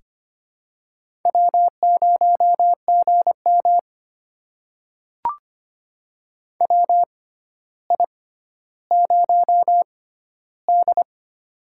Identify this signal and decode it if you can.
Morse code, heard as a keyed tone, about 25 wpm, spaced as for 13 wpm, 700 Hz. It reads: W0GM WI0D
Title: USA Call Signs - 4 Characters - 6x Character Spacing - 25wpm